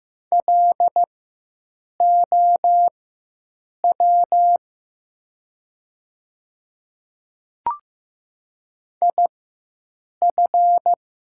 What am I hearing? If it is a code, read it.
Morse code, heard as a keyed tone, about 15 wpm, spaced as for 7 wpm, 700 Hz. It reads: LOW IF
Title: 4x Character Spacing - Top 200 Words - 15wpm